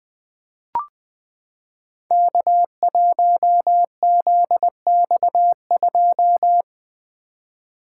Morse code, heard as a keyed tone, about 20 wpm, 700 Hz. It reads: K1ZX2
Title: Code Groups - 5 Letter:Numbers - 20wpm